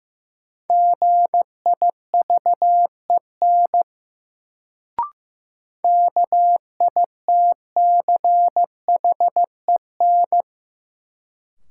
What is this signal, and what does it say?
Morse code, heard as a keyed tone, about 15 wpm, 700 Hz. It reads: GIVEN KITCHEN
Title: Sentences from Top 1400 Words - New Word Drill Alt - 15wpm